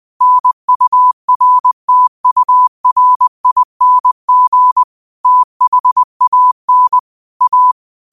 Morse code, heard as a keyed tone, about 20 wpm, 1000 Hz. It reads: NURTURING THAN A
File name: 20wpm